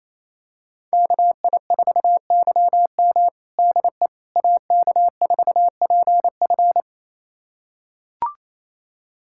Morse code, heard as a keyed tone, about 28 wpm, 690 Hz. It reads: KI4YM DE AK4PF
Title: USA Callsign DE Callsign - 5 Characters - 28wpm